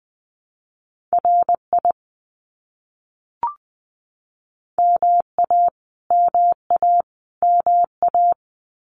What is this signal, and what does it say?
Morse code, heard as a keyed tone, about 20 wpm, 700 Hz. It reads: RI MA MA MA
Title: US States Repeated 3x 20wpm